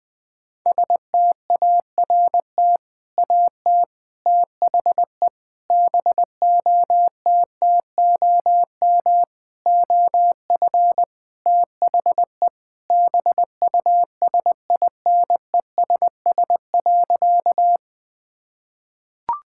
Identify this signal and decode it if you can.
Morse code, heard as a keyed tone, about 21 wpm, 700 Hz. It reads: START AT THE BOTTOM OF THE BUSINESS.
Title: Sentences from Top 1000 Words - Review Part 6 - 20wpm